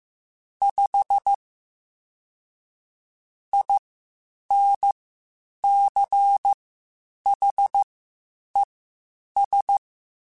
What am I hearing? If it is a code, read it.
Morse code, heard as a keyed tone, about 15 wpm, spaced as for 8 wpm, 800 Hz. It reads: 5 INCHES
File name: (spaced) / QSOs Spaced-mp3s / WA7QOX 1 @13 sp